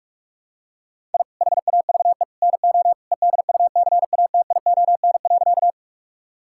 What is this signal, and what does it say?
Morse code, heard as a keyed tone, about 46 wpm, 700 Hz. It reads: I HAVE NO EDUCATION.
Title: Sentences from Top 1000 Words - Review Part 6 - 45wpm